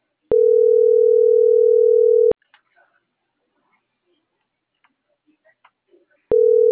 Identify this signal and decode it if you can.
Ringback tone (North America), 440 + 480 Hz.